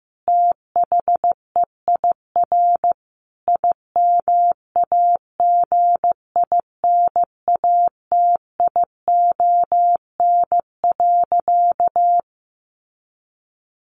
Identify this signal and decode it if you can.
Morse code, heard as a keyed tone, about 15 wpm, 700 Hz. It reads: THEIR IMAGINATION.
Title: Sentences from Top 1300 Words 15wpm